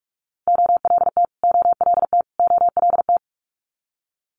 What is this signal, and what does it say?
Morse code, heard as a keyed tone, about 46 wpm, 700 Hz. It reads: OLT OLT OLT
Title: Multi-letter Phonemes Repeated 3x 45wpm